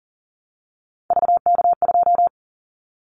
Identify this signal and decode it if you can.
Morse code, heard as a keyed tone, about 38 wpm, 700 Hz. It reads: VK2
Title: Callsign Prefixes for Top 100 DXCC - 40wpm